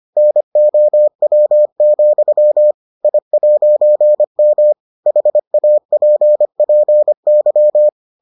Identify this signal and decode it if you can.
Morse code, heard as a keyed tone, about 25 wpm, 600 Hz. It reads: NOW, I'M HAPPY